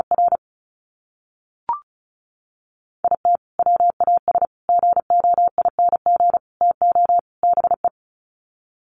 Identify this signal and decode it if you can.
Morse code, heard as a keyed tone, about 34 wpm, 700 Hz. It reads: R IT WAS GOING TO BE
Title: Sets of 5 Words 35wpm